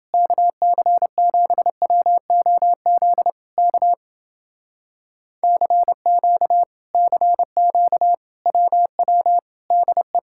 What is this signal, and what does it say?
Morse code, heard as a keyed tone, about 29 wpm, 700 Hz. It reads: KC7WOZ K CQ CQ WW DE